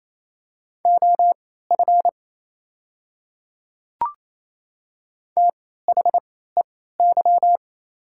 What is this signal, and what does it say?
Morse code, heard as a keyed tone, about 28 wpm, spaced as for 16 wpm, 690 Hz. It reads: OF THEY
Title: 3x Character Spacing - Top 200 Words - 28wpm